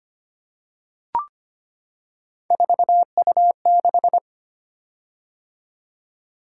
Morse code, heard as a keyed tone, about 25 wpm, 700 Hz. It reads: 4U6
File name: Three Letter-Number Combinations - 25wpm